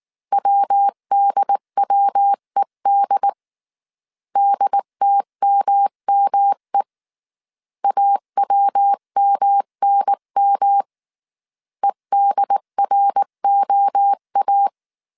Morse code, heard as a keyed tone, about 19 wpm, 800 Hz. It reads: WDWED DTMME AWMNM EDROA